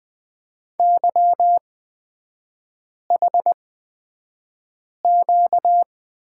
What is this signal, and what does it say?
Morse code, heard as a keyed tone, about 20 wpm, spaced as for 5 wpm, 700 Hz. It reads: YHQ
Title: 255 Lesson 41e - Period - Three Characters 20wpm